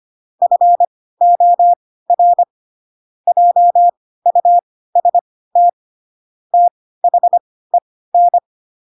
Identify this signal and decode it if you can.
Morse code, heard as a keyed tone, about 25 wpm, spaced as for 16 wpm, 700 Hz. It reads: FOR JUST THEN
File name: Pinocchio / Pinocchio ch4 10 25 ik1hge